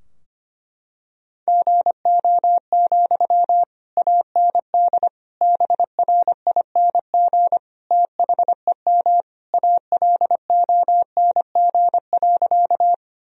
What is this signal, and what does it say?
Morse code, heard as a keyed tone, about 25 wpm, 700 Hz. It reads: GO, AND BRING THEM ALONG.
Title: Sentences from Top 1000 Words - Review Part 3 - 25wpm